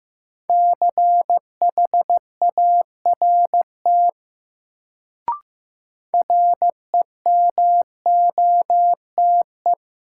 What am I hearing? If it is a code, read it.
Morse code, heard as a keyed tone, about 16 wpm, 700 Hz. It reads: CHART REMOTE
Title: Sentences from Top 1300 Words - New Word Drill Alt - 15wpm